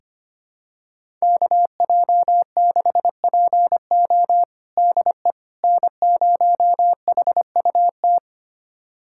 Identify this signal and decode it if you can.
Morse code, heard as a keyed tone, about 25 wpm, 700 Hz. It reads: KJ6PO DE N0HUT
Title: USA Callsign DE Callsign - 5 Characters - 25wpm